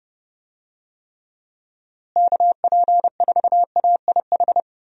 Morse code, heard as a keyed tone, about 30 wpm, 700 Hz. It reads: KP4AIH
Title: DXCC-100 Call Signs - 6 Characters - Repeated 2x - 30wpm